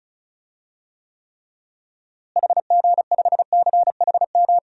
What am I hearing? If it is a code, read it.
Morse code, heard as a keyed tone, about 36 wpm, 700 Hz. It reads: HG5CHM
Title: DXCC-100 Call Signs - 6 Characters - Repeated 3x - 35wpm